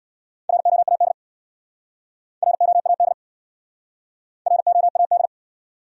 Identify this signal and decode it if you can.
Morse code, heard as a keyed tone, about 68 wpm, 700 Hz. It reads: ROAD ROAD ROAD